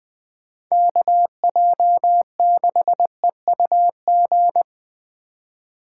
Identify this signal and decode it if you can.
Morse code, heard as a keyed tone, about 20 wpm, 700 Hz. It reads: KJ6EUG